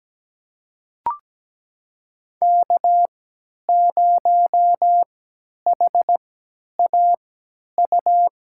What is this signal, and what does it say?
Morse code, heard as a keyed tone, about 17 wpm, spaced as for 10 wpm, 700 Hz. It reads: K0HAU
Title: USA Call Signs - 5 Characters - 3x Character Spacing - 17wpm